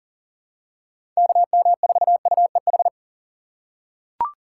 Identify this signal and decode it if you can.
Morse code, heard as a keyed tone, about 40 wpm, 700 Hz. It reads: KM4UEH
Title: USA Call Signs - 6 Characters - 4x Character Spacing - 40wpm